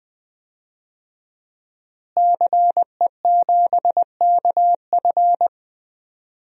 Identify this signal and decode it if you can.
Morse code, heard as a keyed tone, about 20 wpm, 700 Hz. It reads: CE7KF
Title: DXCC-100 Call Signs - 5 Characters - Repeated 3x - 20wpm